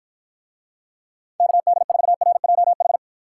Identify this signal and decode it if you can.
Morse code, heard as a keyed tone, about 53 wpm, 700 Hz. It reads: KD4RJH